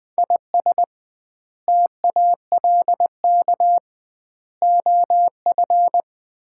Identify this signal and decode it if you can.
Morse code, heard as a keyed tone, about 21 wpm, 700 Hz. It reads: IS TALK OF
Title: 2x Word-Spacing - Sets of 5 Words - 20wpm